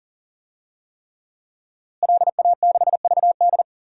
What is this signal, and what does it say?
Morse code, heard as a keyed tone, about 40 wpm, 700 Hz. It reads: LA6VD
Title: DXCC-100 Call Signs - 5 Characters - Repeated 3x - 40wpm